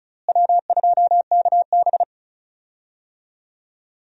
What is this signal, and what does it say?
Morse code, heard as a keyed tone, about 36 wpm, 700 Hz. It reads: W2KB
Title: USA Call Signs - 4 Characters - Repeated 2x - 35wpm